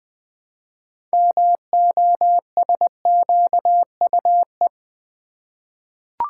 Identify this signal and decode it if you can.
Morse code, heard as a keyed tone, about 20 wpm, 700 Hz. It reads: MOSQUE
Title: 6 Letter Words - First 3 Letters Given - 20wpm